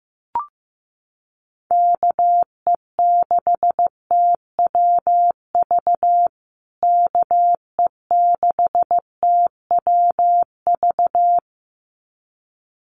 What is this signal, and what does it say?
Morse code, heard as a keyed tone, about 15 wpm, 700 Hz. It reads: KE6TWV KE6TWV
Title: USA Call Signs Repeated 2x 15wpm